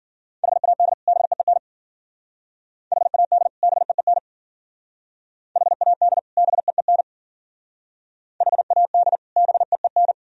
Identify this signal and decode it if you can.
Morse code, heard as a keyed tone, about 48 wpm, 700 Hz. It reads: HAD BEEN HAD BEEN HAD BEEN HAD BEEN
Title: Speed-Racing - Sets of 2 Words 40wpm